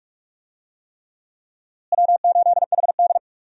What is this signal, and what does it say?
Morse code, heard as a keyed tone, about 46 wpm, 700 Hz. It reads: W8HD
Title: USA Call Signs - 4 Characters - Repeated 3x - 45wpm